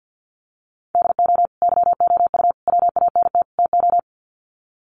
Morse code, heard as a keyed tone, about 50 wpm, 700 Hz. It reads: DO YOU WANT TO
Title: Sets of 4 Words 50wpm